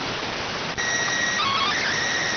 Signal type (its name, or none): SSTV (slow-scan television)